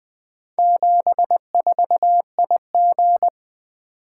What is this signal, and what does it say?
Morse code, heard as a keyed tone, about 20 wpm, 700 Hz. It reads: 74IG
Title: Four Letter-Number Combinations - 20wpm